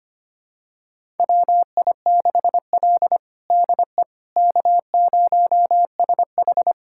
Morse code, heard as a keyed tone, about 25 wpm, 700 Hz. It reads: WI6L DE K0SH
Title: USA Callsign DE Callsign - 4 Characters - 25wpm